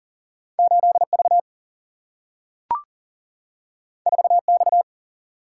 Morse code, heard as a keyed tone, about 40 wpm, 700 Hz. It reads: 8V 4X
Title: Single-Letter Single-Number 40wpm